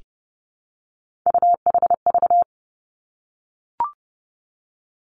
Morse code, heard as a keyed tone, about 30 wpm, 700 Hz. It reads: UHV
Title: Three Challenging Letter-Number Combinations 30wpm